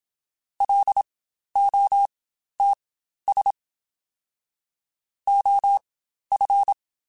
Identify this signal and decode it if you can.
Morse code, heard as a keyed tone, about 27 wpm, spaced as for 12 wpm, 800 Hz. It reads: LOTS OF
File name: Spaced Characters / Spaced Characters-mp3s / Weather 5 12 sp @25